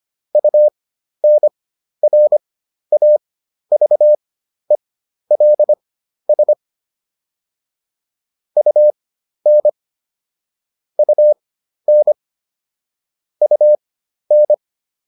Morse code, heard as a keyed tone, about 25 wpm, spaced as for 12 wpm, 600 Hz. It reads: UNRAVELS UN UN UN